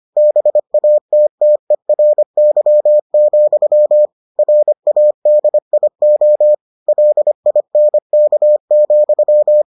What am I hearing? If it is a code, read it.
Morse code, heard as a keyed tone, about 25 wpm, 600 Hz. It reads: BATTERY, RADIO LINK,